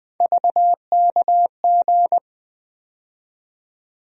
Morse code, heard as a keyed tone, about 20 wpm, 700 Hz. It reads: VKG
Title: USA Call Sign Postfixes - 3 Letters 20wpm